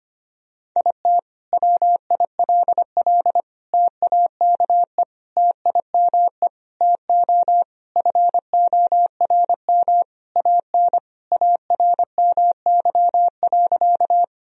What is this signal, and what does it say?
Morse code, heard as a keyed tone, about 25 wpm, 700 Hz. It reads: IT WILL TAKE TIME TO FORM AN ARMY.
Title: Sentences from Top 1000 Words - Review Part 1 - 25wpm